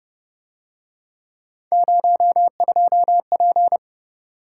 Morse code, heard as a keyed tone, about 31 wpm, 700 Hz. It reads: 02P
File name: Speed-Racing - Three Letter-Number Combinations - 25wpm